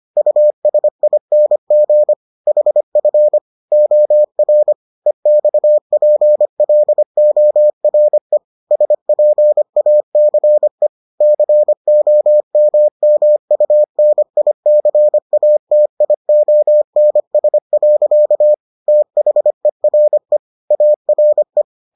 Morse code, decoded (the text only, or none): USING HF OR EXPLORE SPACE COMMUNICATIONS. THERE ARE